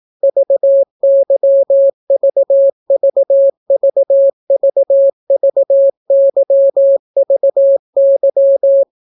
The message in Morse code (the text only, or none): VYVVVVVYVY